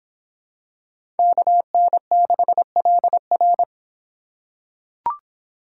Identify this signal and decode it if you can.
Morse code, heard as a keyed tone, about 24 wpm, 700 Hz. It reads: KN6LR